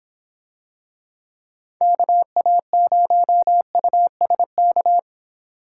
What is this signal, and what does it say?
Morse code, heard as a keyed tone, about 26 wpm, 700 Hz. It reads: KA0USK